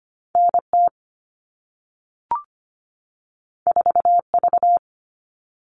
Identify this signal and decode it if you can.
Morse code, heard as a keyed tone, about 25 wpm, 700 Hz. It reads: N T 4 V
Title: All Cut Numbers - 2-Digit 25wpm